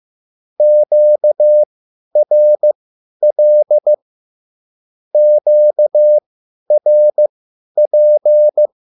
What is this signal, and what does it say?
Morse code, heard as a keyed tone, about 15 wpm, spaced as for 10 wpm, 600 Hz. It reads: QRL QRP